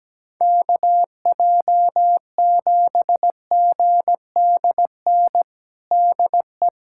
Morse code, heard as a keyed tone, about 17 wpm, 700 Hz. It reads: KJ7GDN DE
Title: USA Callsign DE Callsign - 6 Characters - 17wpm